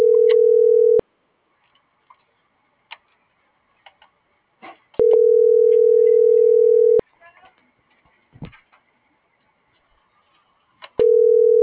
Ringback tone (North America), 440 + 480 Hz.